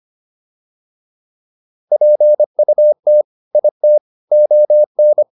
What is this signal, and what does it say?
Morse code, heard as a keyed tone, about 25 wpm, 600 Hz. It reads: PUT IT ON